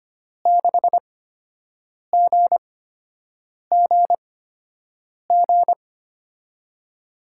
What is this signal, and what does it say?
Morse code, heard as a keyed tone, about 25 wpm, spaced as for 7 wpm, 700 Hz. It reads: 6GGG
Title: USA Call Signs - 6 Characters - 8x Character Spacing - 25wpm